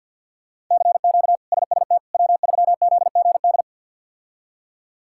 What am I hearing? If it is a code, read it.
Morse code, heard as a keyed tone, about 50 wpm, 700 Hz. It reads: CQ SST W3ZGD
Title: K1USN SST QSO Practice - 50wpm